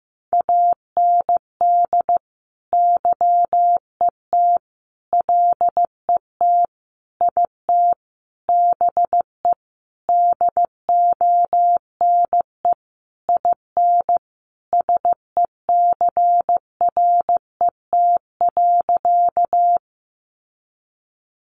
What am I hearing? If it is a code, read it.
Morse code, heard as a keyed tone, about 15 wpm, 700 Hz. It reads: AND YET LET IT BE DONE IN SECRET.
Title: Sentences from Top 1200 Words 15wpm